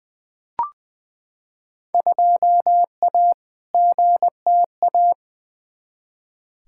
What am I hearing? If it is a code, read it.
Morse code, heard as a keyed tone, about 20 wpm, 700 Hz. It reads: 2A GTA
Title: Field Day Exchanges Encore - 20wpm